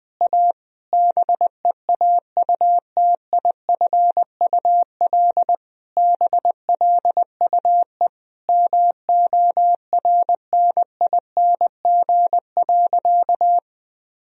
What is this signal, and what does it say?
Morse code, heard as a keyed tone, about 21 wpm, 700 Hz. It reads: A BEAUTIFUL BLUE MORNING.
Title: Sentences from Top 1000 Words - Review Part 3 - 20wpm